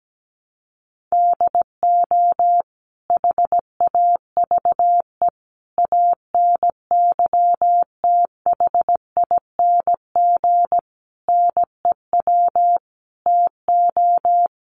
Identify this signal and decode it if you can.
Morse code, heard as a keyed tone, about 17 wpm, 700 Hz. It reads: DO HAVE ANYTHING NEW TO
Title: Sentences from Top 1300 Words 17wpm